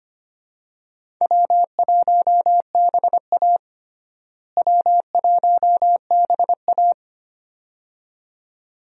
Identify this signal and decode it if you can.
Morse code, heard as a keyed tone, about 25 wpm, 700 Hz. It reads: W1BA W1BA
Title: USA Call Signs - 4 Characters - Repeated 2x - 25wpm